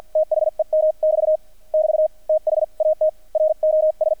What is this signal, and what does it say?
Morse code, heard as a keyed tone, about 48 wpm, spaced as for 41 wpm, 620 Hz. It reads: THEM<BT> <BT> THAT ROS
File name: archived / CRSnet